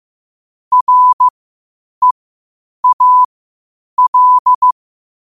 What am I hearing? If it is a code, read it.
Morse code, heard as a keyed tone, about 15 wpm, spaced as for 8 wpm, 1000 Hz. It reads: REAL
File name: qotdfiles / 8wpm